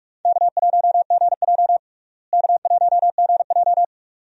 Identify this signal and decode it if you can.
Morse code, heard as a keyed tone, about 42 wpm, 700 Hz. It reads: K1GJ K1GJ
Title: USA Call Signs - 4 Characters - Repeated 2x - 45wpm